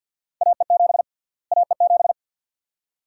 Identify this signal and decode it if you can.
Morse code, heard as a keyed tone, about 46 wpm, 700 Hz. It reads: AE7 AE7
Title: USA Call Sign Prefixes - Repeated 2x - 50wpm